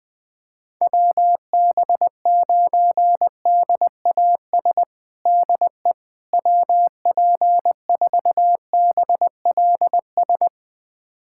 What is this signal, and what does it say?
Morse code, heard as a keyed tone, about 20 wpm, 700 Hz. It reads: WB9DAS DE WP4BLS
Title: USA Callsign DE Callsign - 6 Characters - 20wpm